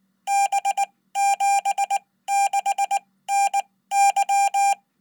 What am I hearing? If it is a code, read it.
Morse code, heard as a keyed tone, about 19 wpm, spaced as for 15 wpm, 800 Hz. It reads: B76NY
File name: Code